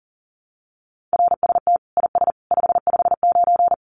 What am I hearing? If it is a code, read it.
Morse code, heard as a keyed tone, about 40 wpm, 700 Hz. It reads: RST IS 559
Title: Common QSO Phrases 40wpm